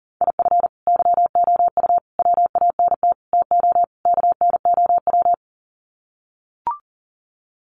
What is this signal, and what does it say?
Morse code, heard as a keyed tone, about 40 wpm, 700 Hz. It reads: IF YOU WANT TO KNOW
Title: Sets of 5 Words Encore2 40wpm